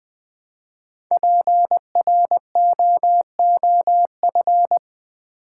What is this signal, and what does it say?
Morse code, heard as a keyed tone, about 20 wpm, 700 Hz. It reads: PROOF
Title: Build A Word - 1 Letter At A Time - 5-Letter Words - 20wpm